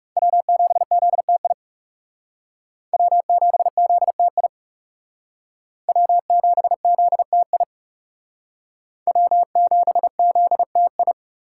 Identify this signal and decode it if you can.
Morse code, heard as a keyed tone, about 38 wpm, 700 Hz. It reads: W7ZTI W7ZTI W7ZTI W7ZTI